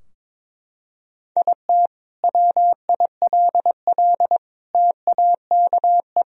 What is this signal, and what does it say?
Morse code, heard as a keyed tone, about 22 wpm, 700 Hz. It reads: IT WILL TAKE
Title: Sentences from Top 1000 Words - Review Part 4 - 22wpm